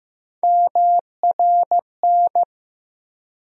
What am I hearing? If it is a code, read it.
Morse code, heard as a keyed tone, about 15 wpm, 700 Hz. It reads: MRN
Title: USA Call Sign Postfixes - 3 Letters 15wpm